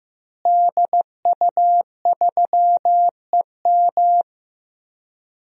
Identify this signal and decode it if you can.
Morse code, heard as a keyed tone, about 15 wpm, 700 Hz. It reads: DU3EM
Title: DXCC-100 Call Signs - 5 Characters - Repeated 3x - 15wpm